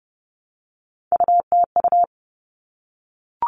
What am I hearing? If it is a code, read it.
Morse code, heard as a keyed tone, about 30 wpm, 700 Hz. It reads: UTU
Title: All Cut Numbers - 3-Digit 30wpm